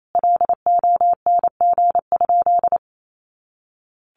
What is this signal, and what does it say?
Morse code, heard as a keyed tone, about 28 wpm, 700 Hz. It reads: LONG?
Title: Sentences from Top 200 Words 28wpm